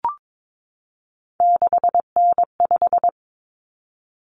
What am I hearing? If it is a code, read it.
Morse code, heard as a keyed tone, about 22 wpm, 700 Hz. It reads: 6N5
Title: Common Cut Numbers - 3-Digit 22wpm